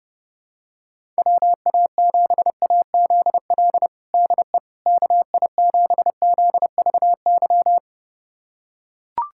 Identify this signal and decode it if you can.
Morse code, heard as a keyed tone, about 30 wpm, 700 Hz. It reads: WA7AZL DE KI7ZVY
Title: USA Callsign DE Callsign - 6 Characters - 30wpm